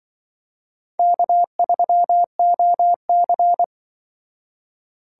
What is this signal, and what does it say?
Morse code, heard as a keyed tone, about 24 wpm, 700 Hz. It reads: K3OC
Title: Speed Racing - USA 1x2 Call Signs 20wpm